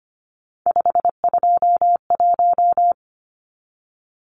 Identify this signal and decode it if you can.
Morse code, heard as a keyed tone, about 25 wpm, 700 Hz. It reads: 521
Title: Numbers - 3-Digit 25wpm